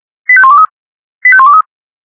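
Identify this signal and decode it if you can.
Five-tone selective call (ZVEI), heard as quick runs of five tones, tones of about 80 ms. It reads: 85123 85123
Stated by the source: ZVEI five-tone call received by SDR.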